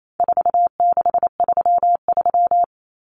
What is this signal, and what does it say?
Morse code, heard as a keyed tone, about 28 wpm, 700 Hz. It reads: 4633
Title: Common Cut Numbers - 4-Digit 28wpm